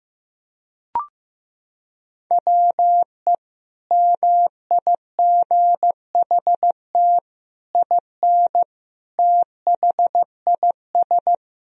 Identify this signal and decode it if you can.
Morse code, heard as a keyed tone, about 15 wpm, 700 Hz. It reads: WE MIGHT IN THIS
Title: Sentences from Top 1000 Words - Review Part 1 - 15wpm